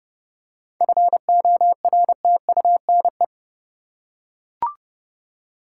Morse code, heard as a keyed tone, about 30 wpm, 700 Hz. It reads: FORTUNE
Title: Seven Letter Words - 30wpm